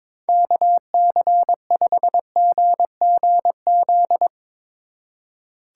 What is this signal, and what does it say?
Morse code, heard as a keyed tone, about 22 wpm, 700 Hz. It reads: KC5GGZ
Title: Speed Racing - USA 2x3 Call Signs 22wpm